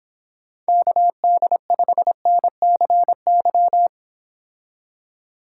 Morse code, heard as a keyed tone, about 26 wpm, 700 Hz. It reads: KD5NCY